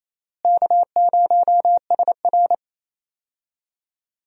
Morse code, heard as a keyed tone, about 28 wpm, 700 Hz. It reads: K0SR
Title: USA Call Signs - 4 Characters - Repeated 2x - 28wpm